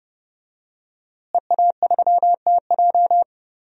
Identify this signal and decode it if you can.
Morse code, heard as a keyed tone, about 30 wpm, 700 Hz. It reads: EA3TJ
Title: DXCC-100 Call Signs - 5 Characters - Repeated 3x - 30wpm